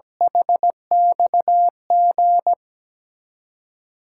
Morse code, heard as a keyed tone, about 17 wpm, 700 Hz. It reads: HXG
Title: USA Call Signs - 8x Character Spacing - 17wpm